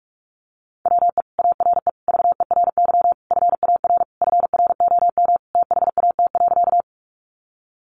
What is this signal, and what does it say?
Morse code, heard as a keyed tone, about 45 wpm, 700 Hz. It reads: WE ARE VERY FAR FROM THAT.
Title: Sentences from Top 300 Words 45wpm